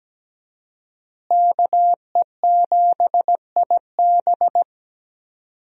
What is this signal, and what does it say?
Morse code, heard as a keyed tone, about 17 wpm, 700 Hz. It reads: KE7IB
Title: USA Call Signs - 5 Characters - Repeated 3x - 17wpm